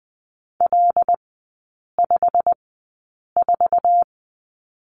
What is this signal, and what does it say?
Morse code, heard as a keyed tone, about 20 wpm, spaced as for 8 wpm, 700 Hz. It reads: L54
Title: 087 Lesson 15e - Number 5 - Three Characters 20wpm